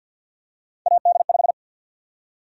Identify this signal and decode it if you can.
Morse code, heard as a keyed tone, about 52 wpm, 700 Hz. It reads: AD5
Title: USA Call Sign Prefixes - 50wpm